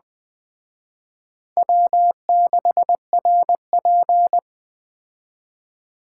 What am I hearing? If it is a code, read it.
Morse code, heard as a keyed tone, about 20 wpm, 700 Hz. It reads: W6RP